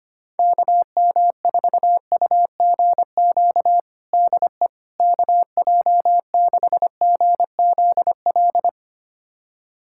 Morse code, heard as a keyed tone, about 25 wpm, 700 Hz. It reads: KM4UGQ DE KJ6GZL
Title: USA Callsign DE Callsign 25wpm